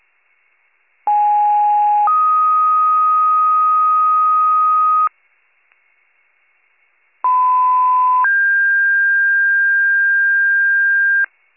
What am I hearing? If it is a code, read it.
Two-tone page, heard as a one-second tone, then a three-second tone, about 830 then 1250 Hz, about 1010 then 1600 Hz. It is heard from 1.1 to 11.2 s.